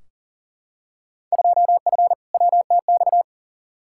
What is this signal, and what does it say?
Morse code, heard as a keyed tone, about 40 wpm, 700 Hz. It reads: 2F WTX
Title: Field Day Exchanges - 40wpm